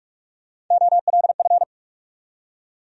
Morse code, heard as a keyed tone, about 46 wpm, 700 Hz. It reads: OLF
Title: USA Call Sign Postfixes - 3 Letters 45wpm